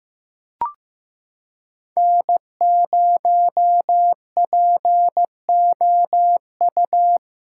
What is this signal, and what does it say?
Morse code, heard as a keyed tone, about 15 wpm, 700 Hz. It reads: N0POU
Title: USA Callsign DE Callsign - 5 Characters - 15wpm